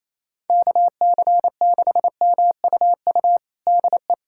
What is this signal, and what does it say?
Morse code, heard as a keyed tone, about 27 wpm, 700 Hz. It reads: KC6MUU DE
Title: USA Callsign DE Callsign - 6 Characters - 28wpm